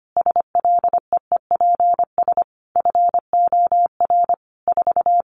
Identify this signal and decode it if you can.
Morse code, heard as a keyed tone, about 25 wpm, 700 Hz. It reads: SLEEPS FOR 4
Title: Additional Easy Phrases 25wpm